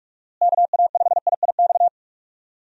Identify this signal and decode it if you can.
Morse code, heard as a keyed tone, about 42 wpm, 700 Hz. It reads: KA5IIX
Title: USA Call Signs - 6 Characters - Repeated 3x - 45wpm